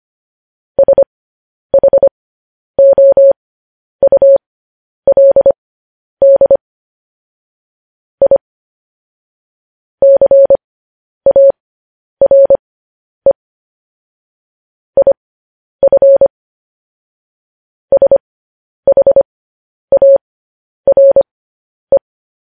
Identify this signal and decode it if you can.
Morse code, heard as a keyed tone, about 25 wpm, spaced as for 10 wpm, 550 Hz. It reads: SHOULD I CARE IF SHARE
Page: news headlines in morse code at 10 wpm